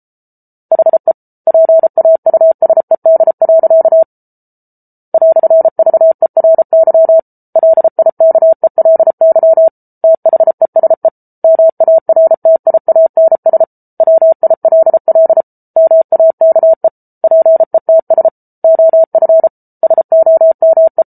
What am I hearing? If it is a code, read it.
Morse code, heard as a keyed tone, about 34 wpm, 650 Hz. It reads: HE PAUSED. "VERY LIKELY THESE MARTIANS WILL MAKE PETS OF SOME